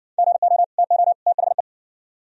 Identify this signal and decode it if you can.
Morse code, heard as a keyed tone, about 59 wpm, 700 Hz. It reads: GO TO THE